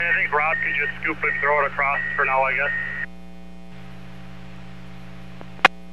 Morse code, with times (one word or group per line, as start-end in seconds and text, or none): none